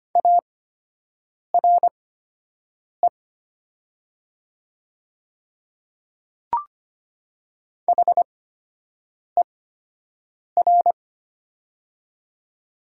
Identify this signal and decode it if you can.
Morse code, heard as a keyed tone, about 25 wpm, spaced as for 7 wpm, 700 Hz. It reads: ARE HER
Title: 8x Character Spacing - Top 100 Words - 25wpm